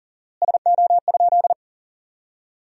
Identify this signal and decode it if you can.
Morse code, heard as a keyed tone, about 40 wpm, 700 Hz. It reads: SO?
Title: 4x Word-Spacing - Sentences from Top 300 Words - 40wpm